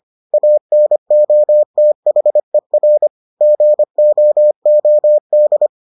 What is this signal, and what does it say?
Morse code, heard as a keyed tone, about 25 wpm, 600 Hz. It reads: ANOTHER GOOD